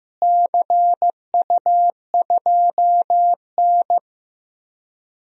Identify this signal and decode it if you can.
Morse code, heard as a keyed tone, about 15 wpm, 700 Hz. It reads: CU2N
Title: DXCC-100 Call Signs - 4 Characters - Repeated 2x - 15wpm